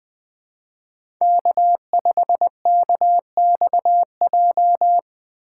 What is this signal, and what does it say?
Morse code, heard as a keyed tone, about 20 wpm, 700 Hz. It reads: K5KXJ
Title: K1USN SST QSO Practice - 20wpm